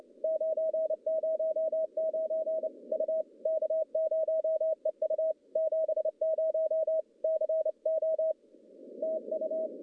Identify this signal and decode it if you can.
Morse code, heard as a keyed tone, about 29 wpm, 640 Hz. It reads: 909 U K0EU 70 CO TU
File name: contests / CW-SS-k5tr- -1200